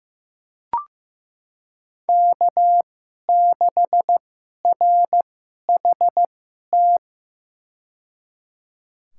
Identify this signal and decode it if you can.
Morse code, heard as a keyed tone, about 15 wpm, spaced as for 11 wpm, 700 Hz. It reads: K6RHT
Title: USA Call Signs - 5 Characters - 2x Character Spacing - 15wpm